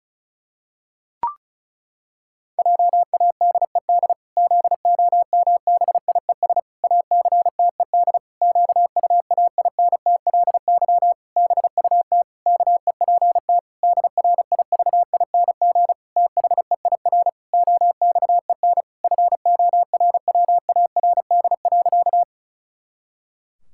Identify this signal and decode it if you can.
Morse code, heard as a keyed tone, about 34 wpm, 700 Hz. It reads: JADED ZOMBIES ACTED QUAINTLY BUT KEPT DRIVING THEIR OXEN FORWARD.
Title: Pangram Challenge v2 - 35wpm